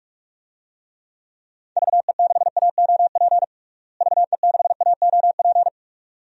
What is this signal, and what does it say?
Morse code, heard as a keyed tone, about 46 wpm, 700 Hz. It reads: VE6AOP VE6AOP
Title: DXCC-100 Call Signs - 6 Characters - Repeated 2x - 45wpm